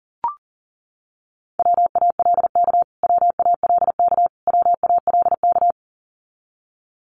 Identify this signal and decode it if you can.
Morse code, heard as a keyed tone, about 40 wpm, 700 Hz. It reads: WALK WALK WALK
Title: Top 500 Words Repeated 40wpm